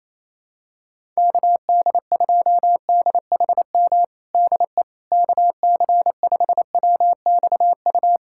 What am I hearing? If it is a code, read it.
Morse code, heard as a keyed tone, about 28 wpm, 700 Hz. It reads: KD2DHM DE KC5WXU
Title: USA Callsign DE Callsign - 6 Characters - 28wpm